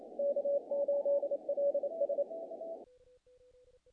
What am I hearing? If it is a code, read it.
Morse code, heard as a keyed tone, about 27 wpm, 570 Hz. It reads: K8LS
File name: contests / naqp-cw-k5tr-op-k5pi- -1800